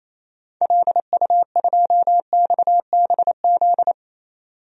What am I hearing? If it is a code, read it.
Morse code, heard as a keyed tone, about 28 wpm, 700 Hz. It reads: LU2XBZ
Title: DXCC-100 Call Signs - 6 Characters - 28wpm